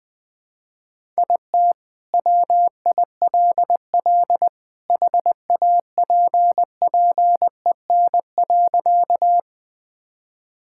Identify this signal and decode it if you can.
Morse code, heard as a keyed tone, about 20 wpm, 700 Hz. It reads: IT WILL HAPPEN.
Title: Sentences with 2-Word Phrases - 20wpm